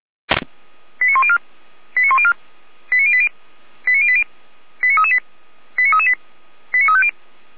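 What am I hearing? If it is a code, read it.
Five-tone selective call (ZVEI), heard as quick runs of five tones, tones of about 70 ms. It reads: 89104 89104 89080 89080 89338 89338 89350
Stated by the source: ZVEI five-tone call received by SDR.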